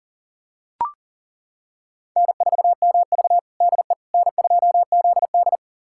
Morse code, heard as a keyed tone, about 37 wpm, 700 Hz. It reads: N4MV DE N2ZD
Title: USA Callsign DE Callsign - 4 Characters - 40wpm